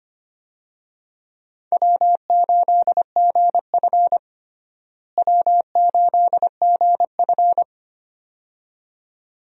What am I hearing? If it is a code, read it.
Morse code, heard as a keyed tone, about 25 wpm, 700 Hz. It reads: W8GF W8GF
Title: USA Call Signs - 4 Characters - Repeated 2x - 25wpm